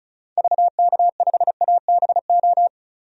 Morse code, heard as a keyed tone, about 36 wpm, 700 Hz. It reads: VK5ABO
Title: DXCC-100 Call Signs - 6 Characters - 35wpm